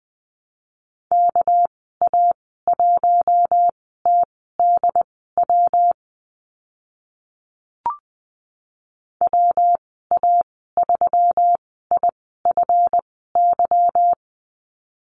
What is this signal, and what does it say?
Morse code, heard as a keyed tone, about 20 wpm, spaced as for 14 wpm, 700 Hz. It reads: KA1TDW WA3IFY
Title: 245 Lesson 39g - Letter z - Call Signs 20wpm